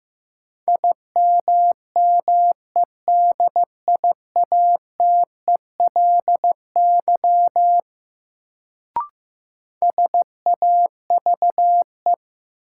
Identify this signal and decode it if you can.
Morse code, heard as a keyed tone, about 15 wpm, 700 Hz. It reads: IMMEDIATELY SAVE
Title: Sentences from Top 1100 Words - New Word Drill - 15wpm